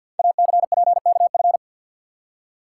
Morse code, heard as a keyed tone, about 53 wpm, 700 Hz. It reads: ACPKF
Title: Code Groups - 5 Letter:Numbers - 50wpm